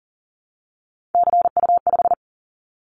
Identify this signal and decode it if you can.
Morse code, heard as a keyed tone, about 43 wpm, 700 Hz. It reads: CU5
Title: Callsign Prefixes for Top 50 DXCC - 40wpm